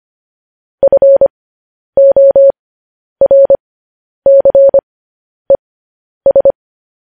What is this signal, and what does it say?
Morse code, heard as a keyed tone, about 25 wpm, spaced as for 10 wpm, 550 Hz. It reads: FORCES